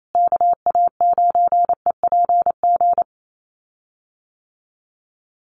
Morse code, heard as a keyed tone, about 28 wpm, 700 Hz. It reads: KA9EPG
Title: USA Call Signs Repeated 3x 28wpm